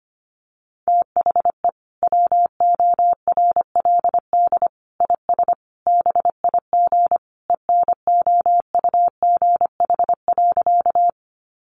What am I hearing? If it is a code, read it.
Morse code, heard as a keyed tone, about 25 wpm, 700 Hz. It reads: THE WORLD IS BIG ENOUGH.
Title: Sentences from Top 400 Words 25wpm